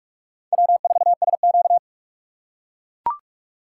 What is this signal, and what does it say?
Morse code, heard as a keyed tone, about 44 wpm, 700 Hz. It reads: W4SQ
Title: USA Call Signs - 4 Characters - 3x Character Spacing - 45wpm